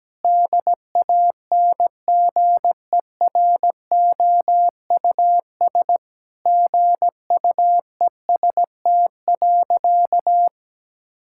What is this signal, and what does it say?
Morse code, heard as a keyed tone, about 17 wpm, 700 Hz. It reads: DANGEROUS GUEST.
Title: Sentences from Top 1600 Words 17wpm